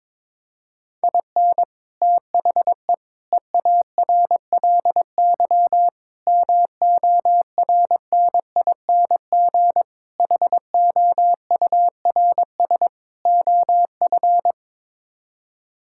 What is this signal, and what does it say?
Morse code, heard as a keyed tone, about 22 wpm, 700 Hz. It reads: IN THE EARLY MORNING HOURS OF
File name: practice / 22wpm / Sets of 6 Words - 22wpm